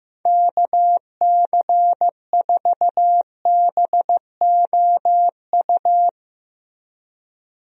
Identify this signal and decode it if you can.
Morse code, heard as a keyed tone, about 15 wpm, 700 Hz. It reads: KC4BOU
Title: USA Call Signs - 6 Characters - Repeated 2x - 15wpm